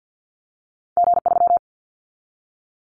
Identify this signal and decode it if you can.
Morse code, heard as a keyed tone, about 50 wpm, 700 Hz. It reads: G3
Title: Callsign Prefixes for Top 25 DXCC - 50wpm